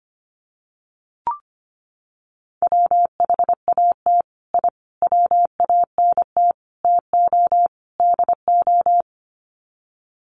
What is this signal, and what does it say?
Morse code, heard as a keyed tone, about 25 wpm, 700 Hz. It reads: WHAT I WANT TO DO
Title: Sets of 5 Words Encore 25wpm